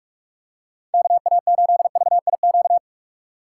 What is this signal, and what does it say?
Morse code, heard as a keyed tone, about 46 wpm, 700 Hz. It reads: KA8VIQ
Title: USA Call Signs - 6 Characters - 4x Character Spacing - 45wpm